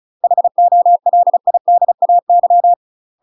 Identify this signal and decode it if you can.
Morse code, heard as a keyed tone, about 34 wpm, 700 Hz. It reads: HOLIDAY
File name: UK MASTER QSOs TRAINING / 35WPM / UK MASTER QSOs 0028